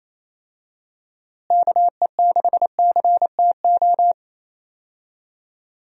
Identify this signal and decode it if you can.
Morse code, heard as a keyed tone, about 28 wpm, 700 Hz. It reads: KE6CTO
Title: USA Call Signs - 6 Characters - Repeated 2x - 28wpm